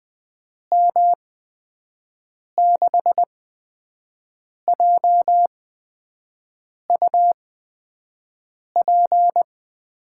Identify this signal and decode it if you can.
Morse code, heard as a keyed tone, about 20 wpm, spaced as for 5 wpm, 700 Hz. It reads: M6JUP
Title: USA Call Signs - 6 Characters - 8x Character Spacing - 20wpm